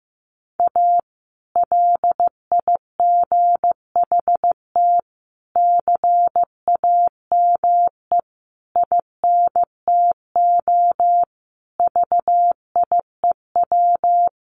Morse code, heard as a keyed tone, about 15 wpm, 700 Hz. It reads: A LIGHT CAME INTO VIEW
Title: Sentences from Top 1100 Words 15wpm